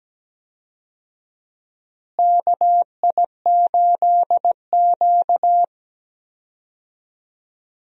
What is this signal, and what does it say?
Morse code, heard as a keyed tone, about 17 wpm, 700 Hz. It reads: KI8Q
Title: USA Call Signs - 4 Characters - Repeated 2x - 17wpm